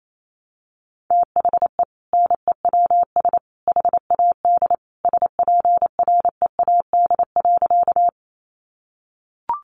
Morse code, heard as a keyed tone, about 28 wpm, 700 Hz. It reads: THE NEWS HAD SPREAD.
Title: Sentences from Top 1200 Words 28wpm